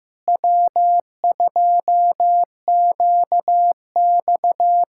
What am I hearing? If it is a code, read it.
Morse code, heard as a keyed tone, about 15 wpm, 700 Hz. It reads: W2QX
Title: USA Callsign DE Callsign - 4 Characters - 15wpm